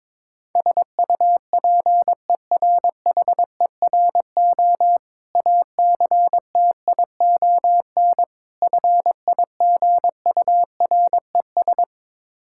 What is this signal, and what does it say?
Morse code, heard as a keyed tone, about 22 wpm, 690 Hz. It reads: SUPERHERO ACTION FIGURES
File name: Difficult Words with Hints - 22wpm